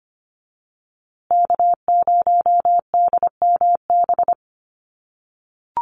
Morse code, heard as a keyed tone, about 25 wpm, 700 Hz. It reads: K0DMB
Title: USA Call Signs Repeated 2x 25wpm